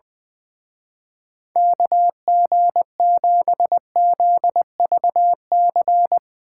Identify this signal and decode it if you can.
Morse code, heard as a keyed tone, about 21 wpm, 700 Hz. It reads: KG7ZVC
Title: Speed Racing - USA 2x3 Call Signs 20wpm